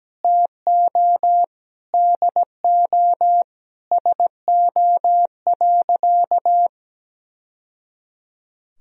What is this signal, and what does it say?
Morse code, heard as a keyed tone, about 17 wpm, 700 Hz. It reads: TO DO SO.
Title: Sentences from Top 1000 Words - Review Part 1 - 17wpm